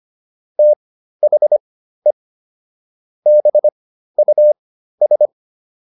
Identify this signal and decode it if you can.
Morse code, heard as a keyed tone, about 25 wpm, spaced as for 13 wpm, 600 Hz. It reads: THE BUS